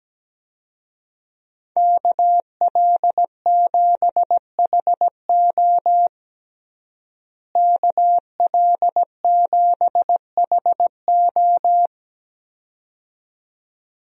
Morse code, heard as a keyed tone, about 17 wpm, 700 Hz. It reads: KL7HO KL7HO
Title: DXCC-100 Call Signs - 5 Characters - Repeated 2x - 17wpm